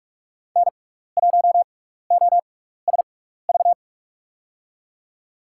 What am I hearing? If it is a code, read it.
Morse code, heard as a keyed tone, about 46 wpm, spaced as for 16 wpm, 700 Hz. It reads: N1OSV
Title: USA Call Signs - 5 Characters - 6x Character Spacing - 45wpm